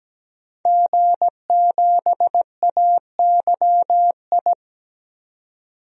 Morse code, heard as a keyed tone, about 17 wpm, 700 Hz. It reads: G7AYI
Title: DXCC-100 Call Signs - 5 Characters - 17wpm